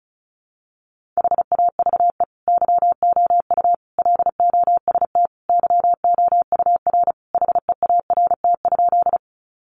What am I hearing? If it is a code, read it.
Morse code, heard as a keyed tone, about 34 wpm, 700 Hz. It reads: HAVE YOU LOST YOUR HEART?
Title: Sentences from Top 1400 Words 35wpm